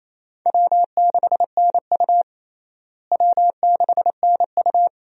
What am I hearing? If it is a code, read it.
Morse code, heard as a keyed tone, about 28 wpm, 700 Hz. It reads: W6NU W6NU
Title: USA Call Signs - 4 Characters - Repeated 3x - 28wpm